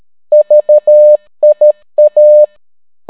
Morse code, heard as a keyed tone, about 13 wpm, 600 Hz. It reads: VIA